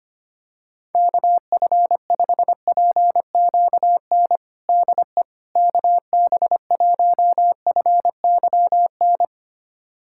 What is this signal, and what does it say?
Morse code, heard as a keyed tone, about 25 wpm, 700 Hz. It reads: KF5PQN DE KB1FYN
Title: USA Callsign DE Callsign - 6 Characters - 25wpm